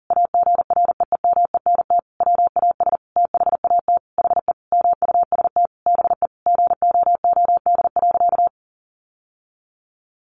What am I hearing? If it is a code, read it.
Morse code, heard as a keyed tone, about 37 wpm, 700 Hz. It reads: AGREEMENT WAS THAT HE MUST BE GOOD.
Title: Sentences from Top 1500 Words 40wpm